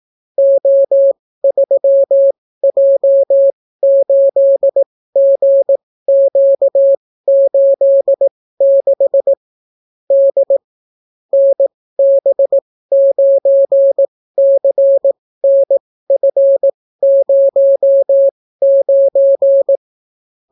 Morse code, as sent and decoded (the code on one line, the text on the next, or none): --- ...-- .--- ---.. --. --.- ---.. -.... / -.. / -. -... ----. -.-. -. ..-. ----- ----.
O3J8GQ86 D NB9CNF09